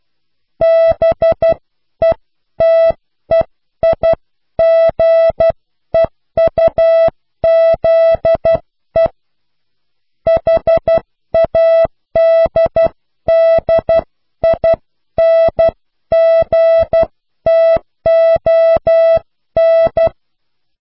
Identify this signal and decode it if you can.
Morse code, heard as a keyed tone, about 12 wpm, 660 Hz. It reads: BETEIGEUZE HADDINGTON
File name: cw / pseudo worden(10)